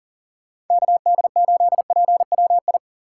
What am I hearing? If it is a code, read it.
Morse code, heard as a keyed tone, about 49 wpm, 700 Hz. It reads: KD8PWI
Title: Speed Racing - USA 2x3 Call Signs 40wpm